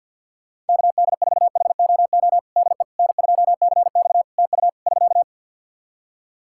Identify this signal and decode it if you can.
Morse code, heard as a keyed tone, about 50 wpm, 700 Hz. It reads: KD4HOO DE N2CX TU <SK>
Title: Practice QSO - Exchange 4 - Ending - 50wpm